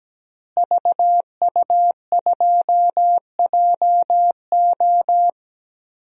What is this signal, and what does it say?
Morse code, heard as a keyed tone, about 17 wpm, 700 Hz. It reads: VU2JO
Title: DXCC-100 Call Signs - 5 Characters - Repeated 3x - 17wpm